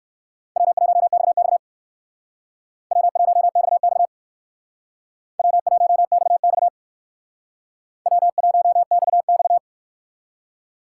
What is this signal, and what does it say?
Morse code, heard as a keyed tone, about 57 wpm, 700 Hz. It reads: W1XX W1XX W1XX W1XX